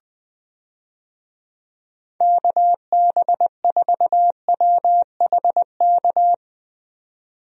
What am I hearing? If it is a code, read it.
Morse code, heard as a keyed tone, about 20 wpm, 700 Hz. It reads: KB4WHK
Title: USA Call Signs - 6 Characters - 20wpm